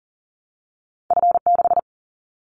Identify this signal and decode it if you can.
Morse code, heard as a keyed tone, about 38 wpm, 700 Hz. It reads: F6
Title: Callsign Prefixes for Top 50 DXCC - 40wpm